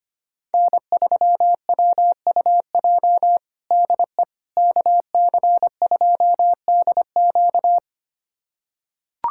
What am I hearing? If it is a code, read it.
Morse code, heard as a keyed tone, about 25 wpm, 700 Hz. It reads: N3WUJ DE KC2DQ
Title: USA Callsign DE Callsign - 5 Characters - 25wpm